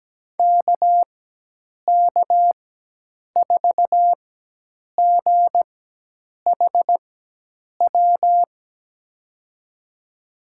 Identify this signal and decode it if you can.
Morse code, heard as a keyed tone, about 17 wpm, spaced as for 8 wpm, 700 Hz. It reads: KK4GHW
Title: USA Call Signs - 4x Character Spacing - 17wpm